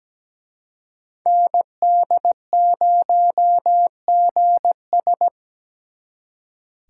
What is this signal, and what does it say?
Morse code, heard as a keyed tone, about 20 wpm, 700 Hz. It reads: ND0GS
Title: Speed Racing - USA Call Signs 17wpm